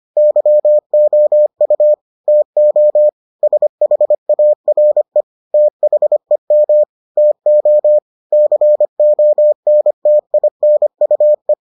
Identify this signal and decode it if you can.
Morse code, heard as a keyed tone, about 25 wpm, 600 Hz. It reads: YOU TO SHARE THEM TO CONTINUE